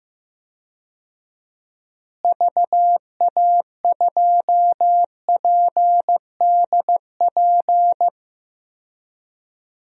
Morse code, heard as a keyed tone, about 15 wpm, 700 Hz. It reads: VA2PDP
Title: DXCC-100 Call Signs - 6 Characters - Repeated 2x - 15wpm